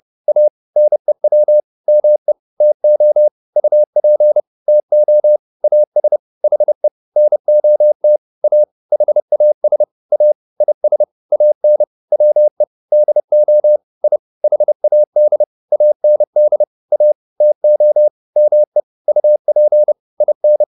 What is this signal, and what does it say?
Morse code, heard as a keyed tone, about 30 wpm, 600 Hz. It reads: A NEW ME TO UP TO AS HE NOT A HAS A IS AN WE DO I HAD AND A TO ME UP IN